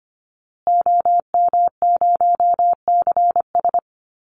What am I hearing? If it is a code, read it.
Morse code, heard as a keyed tone, about 25 wpm, 700 Hz. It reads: OM0CS
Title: Call Signs Repeated 2x - 25wpm